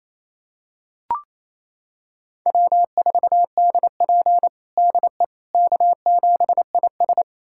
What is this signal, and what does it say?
Morse code, heard as a keyed tone, about 27 wpm, 690 Hz. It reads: W4DP DE K7IS
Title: USA Callsign DE Callsign - 4 Characters - 28wpm